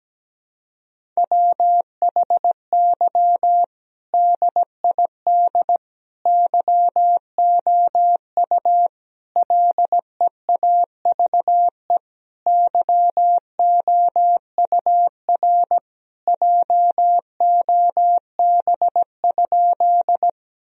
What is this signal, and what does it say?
Morse code, heard as a keyed tone, about 17 wpm, 700 Hz. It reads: WHY DID YOU LEAVE YOUR JOB?
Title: Sentences from Top 1000 Words - Review Part 3 - 17wpm